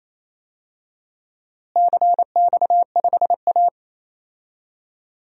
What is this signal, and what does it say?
Morse code, heard as a keyed tone, about 28 wpm, 700 Hz. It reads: CX5A
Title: DXCC-100 Call Signs - 4 Characters - Repeated 3x - 28wpm